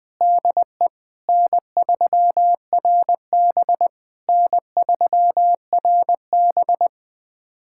Morse code, heard as a keyed tone, about 20 wpm, 690 Hz. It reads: DE N3RB N3RB
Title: SOTA QSO Practice 20wpm